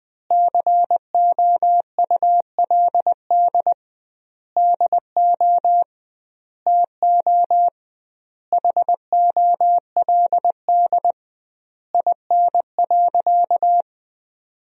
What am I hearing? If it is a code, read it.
Morse code, heard as a keyed tone, about 20 wpm, 700 Hz. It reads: COULD DO TO HOLD IN.
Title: 2x Word-Spacing - Sentences from Top 400 Words - 20wpm